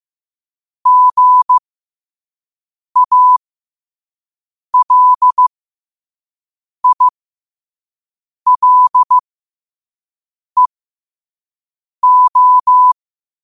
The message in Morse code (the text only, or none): GALILEO